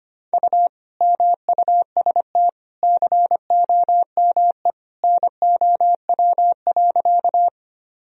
Morse code, heard as a keyed tone, about 25 wpm, 700 Hz. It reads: U MUST COME NOW.
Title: Sentences from Top 200 Words and Abbreviations 25wpm